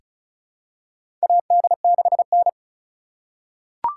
Morse code, heard as a keyed tone, about 34 wpm, 700 Hz. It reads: AD6N
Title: USA Call Signs - 4 Characters - 4x Character Spacing - 35wpm